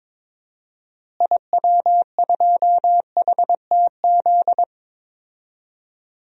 Morse code, heard as a keyed tone, about 22 wpm, 700 Hz. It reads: IW2HTZ
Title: DXCC-100 Call Signs - 6 Characters - Repeated 3x - 22wpm